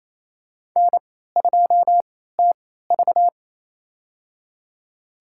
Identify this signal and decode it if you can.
Morse code, heard as a keyed tone, about 28 wpm, spaced as for 16 wpm, 700 Hz. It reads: N2TV
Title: USA Call Signs - 4 Characters - 3x Character Spacing - 28wpm